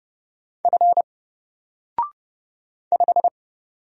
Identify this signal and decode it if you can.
Morse code, heard as a keyed tone, about 30 wpm, 700 Hz. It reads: F 5
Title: Single Character (Letter:Number:Punctuation:Prosign) 30wpm